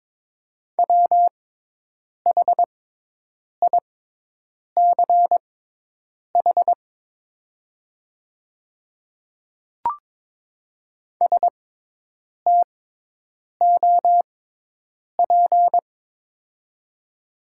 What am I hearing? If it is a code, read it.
Morse code, heard as a keyed tone, about 22 wpm, spaced as for 8 wpm, 700 Hz. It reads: WHICH STOP
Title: 6x Character Spacing - Top 300 Words - 22wpm